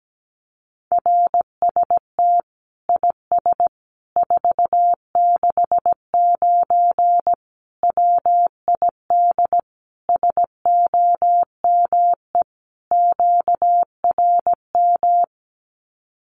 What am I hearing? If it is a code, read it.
Morse code, heard as a keyed tone, about 17 wpm, 700 Hz. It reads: RST IS 469 WID SOME QRM
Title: Common QSO Phrases 17wpm